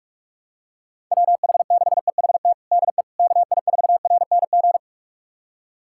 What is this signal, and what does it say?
Morse code, heard as a keyed tone, about 46 wpm, 700 Hz. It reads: WH6EHT DE KI4RNG
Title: USA Callsign DE Callsign 45wpm